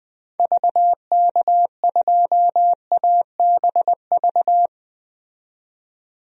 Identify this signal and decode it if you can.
Morse code, heard as a keyed tone, about 20 wpm, 700 Hz. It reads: VK2ABV
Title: DXCC-100 Call Signs - 6 Characters - 20wpm